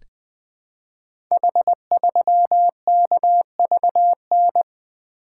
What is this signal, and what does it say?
Morse code, heard as a keyed tone, about 20 wpm, 700 Hz. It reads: H3KVN
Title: Code Groups - 5 Letter:Numbers - 20wpm